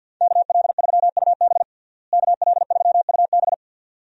Morse code, heard as a keyed tone, about 50 wpm, 700 Hz. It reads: KL3UB KL3UB
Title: USA Call Signs - 5 Characters - Repeated 2x - 50wpm